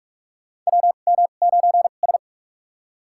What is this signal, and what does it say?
Morse code, heard as a keyed tone, about 46 wpm, spaced as for 33 wpm, 700 Hz. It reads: WM9S
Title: USA Call Signs - 4 Characters - 2x Character Spacing - 45wpm